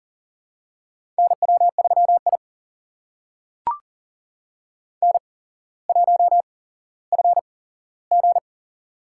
Morse code, heard as a keyed tone, about 40 wpm, spaced as for 17 wpm, 690 Hz. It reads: NW3I N1FG
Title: USA Call Signs - 4 Characters - 8x Character Spacing - 40wpm